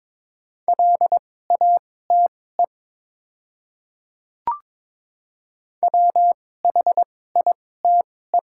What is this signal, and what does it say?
Morse code, heard as a keyed tone, about 22 wpm, spaced as for 16 wpm, 700 Hz. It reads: LATE WHITE
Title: 2x Character Spacing - Top 500 Words - 22wpm